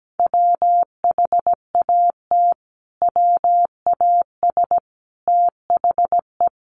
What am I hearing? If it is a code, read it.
Morse code, heard as a keyed tone, about 17 wpm, 700 Hz. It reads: WHAT WAS THE
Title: Sentences from Top 300 Words 17wpm